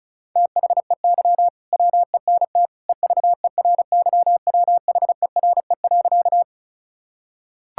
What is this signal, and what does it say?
Morse code, heard as a keyed tone, about 36 wpm, 700 Hz. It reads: THEY WENT EVERYWHERE.
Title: Sentences from Top 1600 Words 35wpm